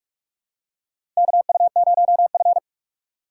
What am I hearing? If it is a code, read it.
Morse code, heard as a keyed tone, about 44 wpm, 700 Hz. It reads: KU0F
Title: USA Call Signs - 4 Characters - 3x Character Spacing - 45wpm